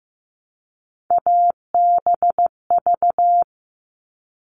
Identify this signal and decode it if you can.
Morse code, heard as a keyed tone, about 15 wpm, 700 Hz. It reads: ABV
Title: All Cut Numbers - 3-Digit 15wpm